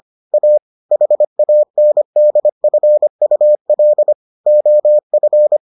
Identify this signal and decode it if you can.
Morse code, heard as a keyed tone, about 25 wpm, 600 Hz. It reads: A HANDFUL OF